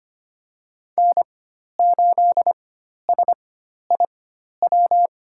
Morse code, heard as a keyed tone, about 25 wpm, spaced as for 12 wpm, 700 Hz. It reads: N8SIW
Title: USA Call Signs - 5 Characters - 4x Character Spacing - 25wpm